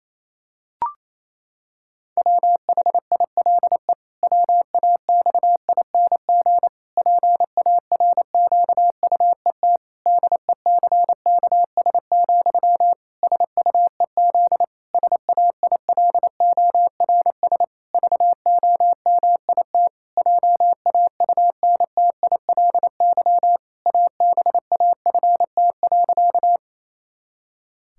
Morse code, heard as a keyed tone, about 28 wpm, 700 Hz. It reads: WHILE WAXING PARQUET DECKS, SUEZ SAILORS VOMIT JAUNTILY ABAFT.
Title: Pangram Challenge v2 - 28wpm